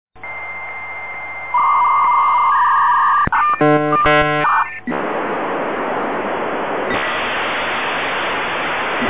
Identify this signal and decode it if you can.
Dial-up modem handshake.
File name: dialup-modem-14400bps-33600bps